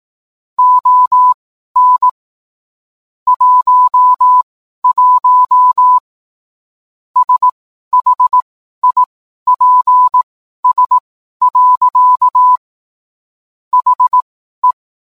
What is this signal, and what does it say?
Morse code, heard as a keyed tone, about 18 wpm, spaced as for 12 wpm, 1000 Hz. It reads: ON 11 SHIPS. HE